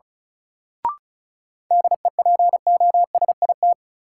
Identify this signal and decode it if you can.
Morse code, heard as a keyed tone, about 36 wpm, 700 Hz. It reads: DEPOSIT
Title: Sentences from Top 1300 Words - New Word Drill - 35wpm